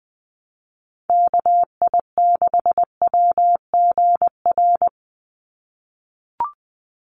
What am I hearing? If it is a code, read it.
Morse code, heard as a keyed tone, about 20 wpm, 700 Hz. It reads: KI6WGR
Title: USA Call Signs 20wpm